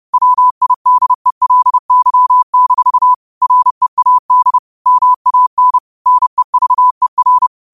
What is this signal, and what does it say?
Morse code, heard as a keyed tone, about 30 wpm, 1000 Hz. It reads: WIDELY- READ MAN NEVER
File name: qotdfiles / 30wpm